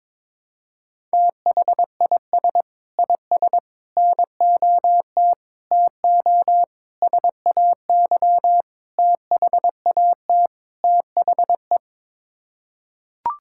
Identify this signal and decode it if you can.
Morse code, heard as a keyed tone, about 22 wpm, 700 Hz. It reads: THIS IS NOT TO SAY THAT THE
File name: practice / 22wpm / Sets of 7 Words - 22wpm